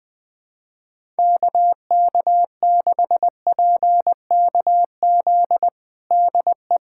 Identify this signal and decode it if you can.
Morse code, heard as a keyed tone, about 20 wpm, 700 Hz. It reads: KK6PKZ DE
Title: USA Callsign DE Callsign 20wpm